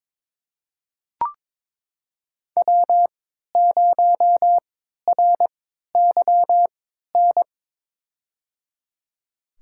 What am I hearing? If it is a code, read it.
Morse code, heard as a keyed tone, about 22 wpm, spaced as for 13 wpm, 700 Hz. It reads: W0RYN
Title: USA Call Signs - 5 Characters - 3x Character Spacing - 22wpm